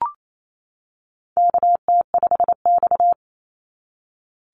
Morse code, heard as a keyed tone, about 28 wpm, 700 Hz. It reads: KT5X
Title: NA SOTA Activator Callsigns 28wpm